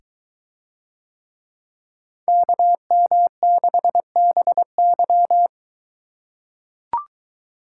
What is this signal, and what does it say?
Morse code, heard as a keyed tone, about 21 wpm, 700 Hz. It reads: KM6BY